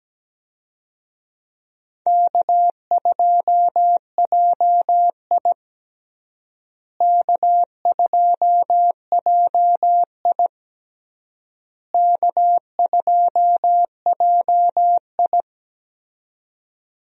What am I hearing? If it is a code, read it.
Morse code, heard as a keyed tone, about 17 wpm, 700 Hz. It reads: K2JI K2JI K2JI
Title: USA Call Signs - 4 Characters - Repeated 3x - 17wpm